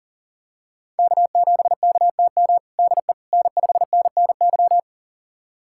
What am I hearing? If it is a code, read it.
Morse code, heard as a keyed tone, about 40 wpm, 700 Hz. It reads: K7KTM DE N5NNY
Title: USA Callsign DE Callsign 40wpm